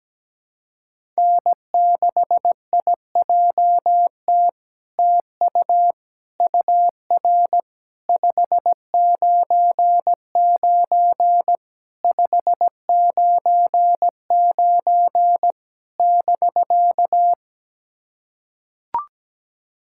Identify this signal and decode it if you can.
Morse code, heard as a keyed tone, about 17 wpm, 700 Hz. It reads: N6IJT TU UR 599 599 <BK>
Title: SOTA QSO Practice 17wpm